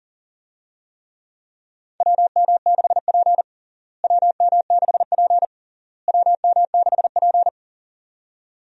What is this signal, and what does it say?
Morse code, heard as a keyed tone, about 40 wpm, 700 Hz. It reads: WM6P WM6P WM6P
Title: USA Call Signs - 4 Characters - Repeated 3x - 40wpm